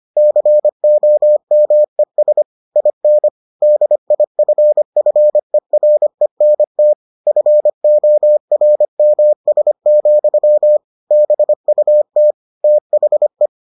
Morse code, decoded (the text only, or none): COMES IN DIFFERENT FORMS, BUT THE